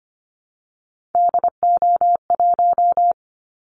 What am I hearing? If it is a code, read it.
Morse code, heard as a keyed tone, about 25 wpm, 700 Hz. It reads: DO1
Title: Callsign Prefixes for Top 50 DXCC - 25wpm